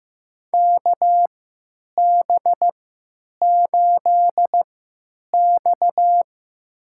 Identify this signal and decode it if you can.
Morse code, heard as a keyed tone, about 15 wpm, spaced as for 9 wpm, 700 Hz. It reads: KB8X
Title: USA Call Signs - 4 Characters - 3x Character Spacing - 15wpm